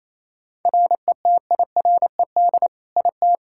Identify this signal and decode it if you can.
Morse code, heard as a keyed tone, about 27 wpm, 700 Hz. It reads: RETIRED IT
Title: Practice QSO - Exchange 3 - More Info - 28wpm